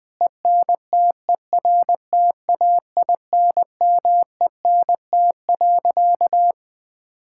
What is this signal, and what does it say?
Morse code, heard as a keyed tone, about 21 wpm, 700 Hz. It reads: ENTERTAINMENT.
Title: Sentences from Top 1000 Words - Review Part 6 - 20wpm